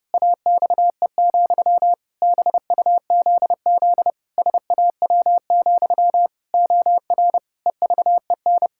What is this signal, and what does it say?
Morse code, heard as a keyed tone, about 30 wpm, 700 Hz. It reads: AXE, BUZZ SAW, OR EVEN
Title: Pangram Challenge v2 - 30wpm